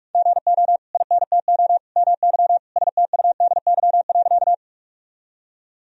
Morse code, heard as a keyed tone, about 45 wpm, 700 Hz. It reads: GO INTO MY STUDY.
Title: Sentences from Top 1000 Words - Review Part 1 - 45wpm